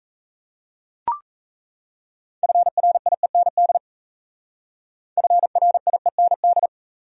Morse code, heard as a keyed tone, about 40 wpm, 700 Hz. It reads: FRIEND FRIEND